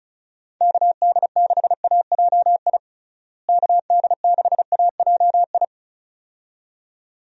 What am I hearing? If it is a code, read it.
Morse code, heard as a keyed tone, about 36 wpm, 700 Hz. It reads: KD6AJI KD6AJI
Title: USA Call Signs - 6 Characters - Repeated 2x - 35wpm